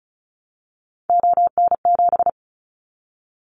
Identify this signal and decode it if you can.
Morse code, heard as a keyed tone, about 36 wpm, 700 Hz. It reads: ON7
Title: Callsign Prefixes for Top 50 DXCC - 35wpm